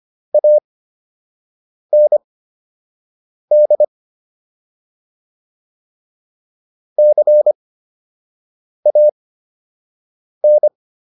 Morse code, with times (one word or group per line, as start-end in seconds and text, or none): 0.3-3.8 AND
7.0-10.7 CAN